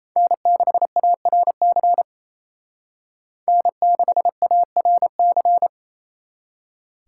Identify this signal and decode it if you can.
Morse code, heard as a keyed tone, about 36 wpm, 700 Hz. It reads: N6ARC N6ARC